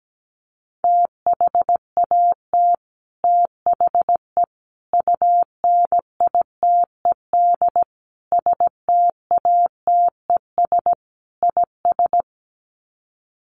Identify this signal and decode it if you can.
Morse code, heard as a keyed tone, about 17 wpm, 690 Hz. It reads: THAT THE UNITED STATES IS
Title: Sets of 5 Words Encore 17wpm